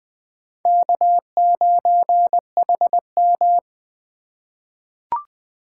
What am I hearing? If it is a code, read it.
Morse code, heard as a keyed tone, about 20 wpm, 700 Hz. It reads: K9HM
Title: USA Call Signs - 4 Characters - 4x Character Spacing - 20wpm